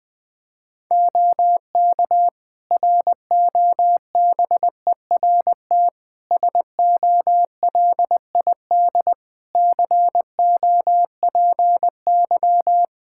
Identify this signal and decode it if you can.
Morse code, heard as a keyed tone, about 20 wpm, 700 Hz. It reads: OK ROBERT SOLID COPY
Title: K1USN SST QSO Practice 20wpm